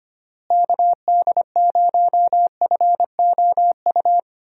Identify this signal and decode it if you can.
Morse code, heard as a keyed tone, about 25 wpm, 700 Hz. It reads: KD0FOU
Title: USA Call Signs - 6 Characters - Repeated 2x - 25wpm